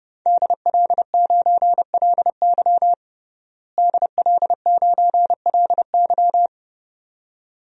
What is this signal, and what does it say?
Morse code, heard as a keyed tone, about 32 wpm, 700 Hz. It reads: DL9LY DL9LY
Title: DXCC-100 Call Signs - 5 Characters - Repeated 2x - 30wpm